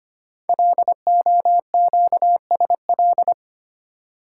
Morse code, heard as a keyed tone, about 25 wpm, 700 Hz. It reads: LOQSL
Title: Code Groups - 5 Letters - 25wpm